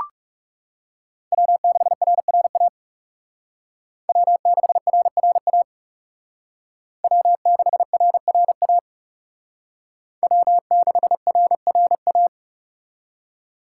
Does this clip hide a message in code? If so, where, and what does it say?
Yes: Morse code from 1.3 to 12.3 s, heard as a keyed tone, about 40 wpm, 700 Hz. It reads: W6RRA W6RRA W6RRA W6RRA